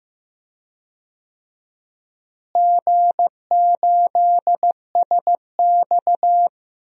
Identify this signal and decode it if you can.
Morse code, heard as a keyed tone, about 15 wpm, 700 Hz. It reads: G8SX
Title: DXCC-100 Call Signs - 4 Characters - Repeated 2x - 15wpm